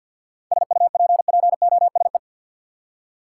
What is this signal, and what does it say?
Morse code, heard as a keyed tone, about 52 wpm, 700 Hz. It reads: SUPPOSE
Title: Seven Letter Words - 50wpm